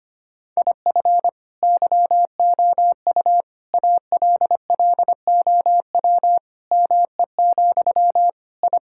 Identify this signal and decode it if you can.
Morse code, heard as a keyed tone, about 25 wpm, 700 Hz. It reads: IF YOU ALLOW ME, I
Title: Sentences from Top 1000 Words - Review Part 5 - 25wpm